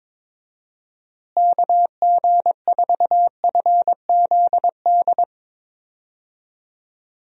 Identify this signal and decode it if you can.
Morse code, heard as a keyed tone, about 22 wpm, 700 Hz. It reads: KG4FZD
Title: USA Call Signs - 6 Characters - Repeated 2x - 22wpm